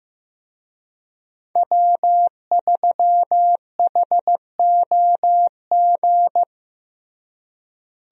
Morse code, heard as a keyed tone, about 16 wpm, 700 Hz. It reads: W3HOG
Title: USA Call Signs - 5 Characters - Repeated 3x - 15wpm